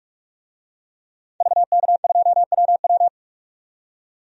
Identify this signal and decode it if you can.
Morse code, heard as a keyed tone, about 46 wpm, 700 Hz. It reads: VK2WW
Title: DXCC-100 Call Signs - 5 Characters - 45wpm